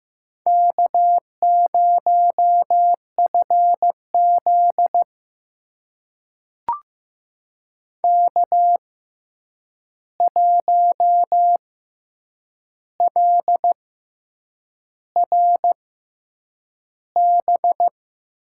Morse code, heard as a keyed tone, about 15 wpm, spaced as for 7 wpm, 700 Hz. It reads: K0FZ K1LRB
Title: USA Call Signs - 6x Character Spacing - 15wpm